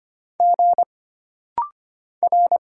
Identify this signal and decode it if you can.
Morse code, heard as a keyed tone, about 25 wpm, 700 Hz. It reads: G R
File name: Single Character (Letter:Number:Punctuation:Prosign) - Rapid-Fire - 25wpm